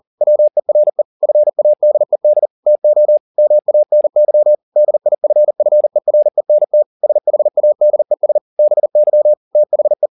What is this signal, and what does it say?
Morse code, heard as a keyed tone, about 40 wpm, 600 Hz. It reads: WERE FADED TO MANY DIFFERENT SHADES BY THE